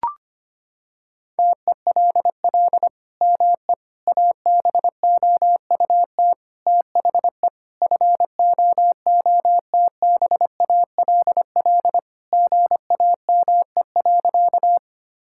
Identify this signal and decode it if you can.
Morse code, heard as a keyed tone, about 25 wpm, 700 Hz. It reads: TELL ME ABOUT THE FOOTBALL GAME.
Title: Sentences from Top 1600 Words 25wpm